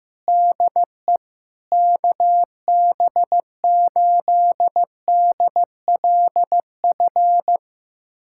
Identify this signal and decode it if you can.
Morse code, heard as a keyed tone, about 15 wpm, 700 Hz. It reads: DE KB8DLF
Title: USA Callsign DE Callsign 15wpm